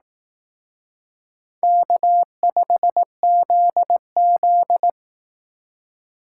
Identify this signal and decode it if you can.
Morse code, heard as a keyed tone, about 18 wpm, 700 Hz. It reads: K5ZZ
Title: Speed Racing - USA 1x2 Call Signs 15wpm